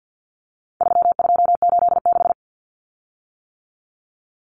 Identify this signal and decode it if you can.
Morse code, heard as a keyed tone, about 50 wpm, 700 Hz. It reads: 3286
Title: Common Cut Numbers - 4-Digit 50wpm